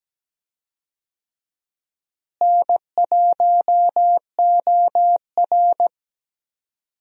Morse code, heard as a keyed tone, about 17 wpm, 700 Hz. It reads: N1OR
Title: USA Call Signs - 4 Characters - Repeated 2x - 17wpm